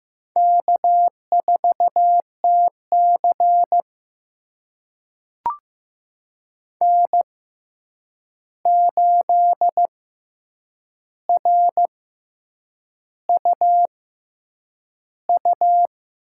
Morse code, heard as a keyed tone, about 15 wpm, spaced as for 7 wpm, 700 Hz. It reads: K4TC N8RUU
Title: USA Call Signs - 6x Character Spacing - 15wpm